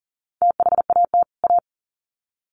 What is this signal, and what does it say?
Morse code, heard as a keyed tone, about 40 wpm, 700 Hz. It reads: THAT A
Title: Sets of 2 Words 40wpm